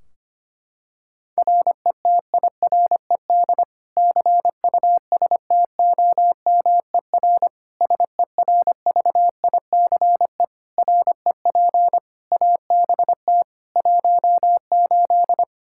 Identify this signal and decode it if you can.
Morse code, heard as a keyed tone, about 25 wpm, 700 Hz. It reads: RETIRED CUSTOMER SERVICE REP ABT 18
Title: Practice QSO - Exchange 3 - More Info - 25wpm